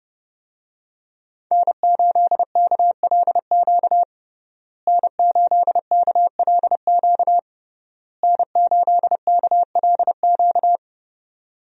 Morse code, heard as a keyed tone, about 30 wpm, 700 Hz. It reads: N8KLQ N8KLQ N8KLQ
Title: USA Call Signs - 5 Characters - Repeated 3x - 30wpm